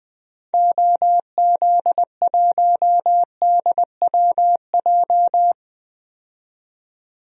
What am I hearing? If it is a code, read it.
Morse code, heard as a keyed tone, about 20 wpm, 700 Hz. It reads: OZ1DWJ
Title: DXCC-100 Call Signs - 6 Characters - 20wpm